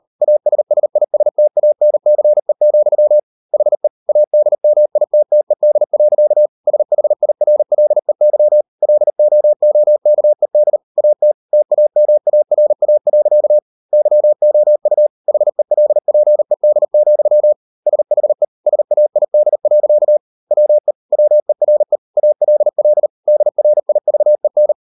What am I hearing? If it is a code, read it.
Morse code, heard as a keyed tone, about 40 wpm, 600 Hz. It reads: ASSISTANCE, HE ADMITTED. SHIRLEY LOOKED AT TAMARA. YOU HELPED, SHE SAID. WE WERE ALL DRIVEN